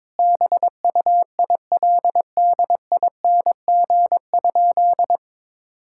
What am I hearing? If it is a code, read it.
Morse code, heard as a keyed tone, about 22 wpm, 700 Hz. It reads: BUILDING?
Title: Sentences from Top 1000 Words - Review Part 6 - 22wpm